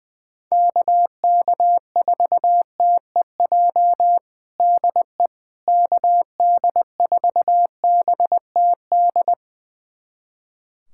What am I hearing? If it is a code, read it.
Morse code, heard as a keyed tone, about 20 wpm, 700 Hz. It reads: KK4TEJ DE KD4BTD
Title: USA Callsign DE Callsign - 6 Characters - 20wpm